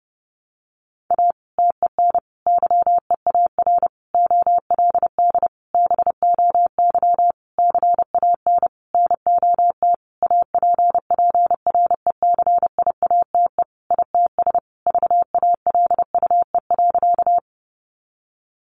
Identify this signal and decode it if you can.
Morse code, heard as a keyed tone, about 30 wpm, 700 Hz. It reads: A TEN YEAR OLD BOY CAN NOT APPRECIATE ITS VALUE.
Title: Sentences from Top 1300 Words 30wpm